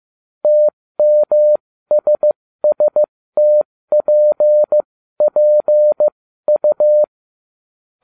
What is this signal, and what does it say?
Morse code, heard as a keyed tone, about 15 wpm, spaced as for 13 wpm, 600 Hz. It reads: TMSSTPPU
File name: cw / ubacw07 1